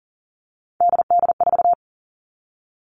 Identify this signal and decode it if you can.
Morse code, heard as a keyed tone, about 40 wpm, 700 Hz. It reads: DD4
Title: Three Challenging Letter-Number Combinations 40wpm